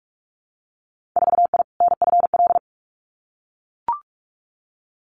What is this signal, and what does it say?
Morse code, heard as a keyed tone, about 44 wpm, 700 Hz. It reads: 4I NFL
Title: Winter Field Day Exchanges 45wpm